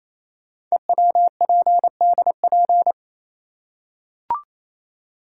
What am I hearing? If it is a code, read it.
Morse code, heard as a keyed tone, about 28 wpm, 700 Hz. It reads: EWPDP
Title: Code Groups - 5 Letters - 28wpm